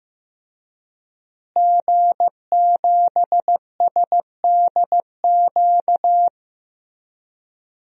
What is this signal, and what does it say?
Morse code, heard as a keyed tone, about 15 wpm, 700 Hz. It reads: G7SDQ
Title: DXCC-100 Call Signs - 5 Characters - Repeated 3x - 15wpm